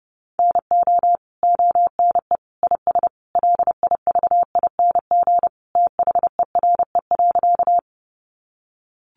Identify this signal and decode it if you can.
Morse code, heard as a keyed tone, about 30 wpm, 700 Hz. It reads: NO ONE IS LIVING THERE.
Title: Sentences from Top 1100 Words 30wpm